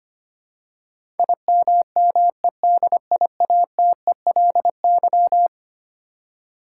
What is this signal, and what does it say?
Morse code, heard as a keyed tone, about 25 wpm, 700 Hz. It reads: IMMEDIATELY
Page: morse-code-ninja-practice-25wpm